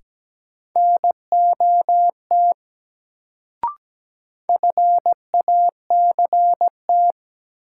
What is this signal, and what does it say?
Morse code, heard as a keyed tone, about 17 wpm, 700 Hz. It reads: NOT FACT
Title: Sets of 2 Words - New Word Drill - 17wpm